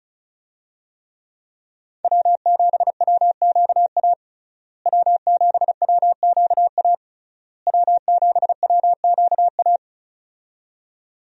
Morse code, heard as a keyed tone, about 36 wpm, 700 Hz. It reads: W7WQA W7WQA W7WQA
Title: USA Call Signs - 5 Characters - Repeated 3x - 35wpm